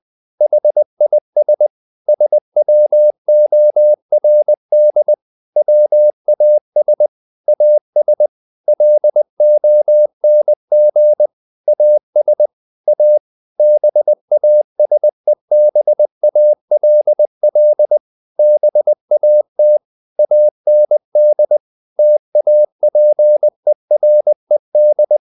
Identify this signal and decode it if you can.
Morse code, heard as a keyed tone, about 20 wpm, 600 Hz. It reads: HIS SWORD WAS AS LONG AS A BASEBALL BAT AND TAPERED
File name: the old man and the sea / the old man and the sea part 1 20wpm